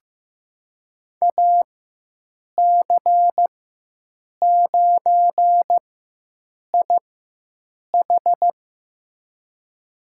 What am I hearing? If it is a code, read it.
Morse code, heard as a keyed tone, about 16 wpm, spaced as for 7 wpm, 700 Hz. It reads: AC9IH
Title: USA Call Signs - 5 Characters - 4x Character Spacing - 15wpm